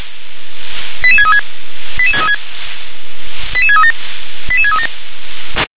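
Five-tone selective call (ZVEI), heard as quick runs of five tones, tones of about 70 ms. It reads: 88636 88636 88637 88637
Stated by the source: ZVEI five-tone selective call received by SDR.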